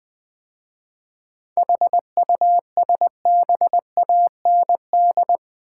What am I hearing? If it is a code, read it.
Morse code, heard as a keyed tone, about 20 wpm, 700 Hz. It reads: HUSBAND
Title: 7 Letter Words - First 3 Letters Given - 20wpm